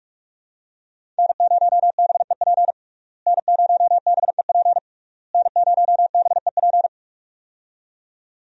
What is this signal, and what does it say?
Morse code, heard as a keyed tone, about 44 wpm, 700 Hz. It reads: N0BEP N0BEP N0BEP
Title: USA Call Signs - 5 Characters - Repeated 3x - 45wpm